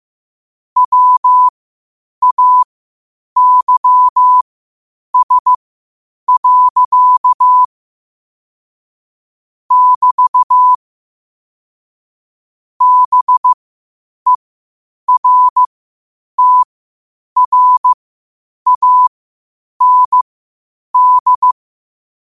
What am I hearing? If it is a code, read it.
Morse code, heard as a keyed tone, about 15 wpm, spaced as for 8 wpm, 1000 Hz. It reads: WAYS. <BT> BERTRAND
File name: qotdfiles / 8wpm